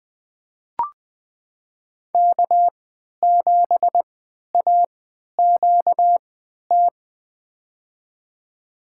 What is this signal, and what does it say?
Morse code, heard as a keyed tone, about 20 wpm, spaced as for 11 wpm, 700 Hz. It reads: K7AQT
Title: USA Call Signs - 5 Characters - 3x Character Spacing - 20wpm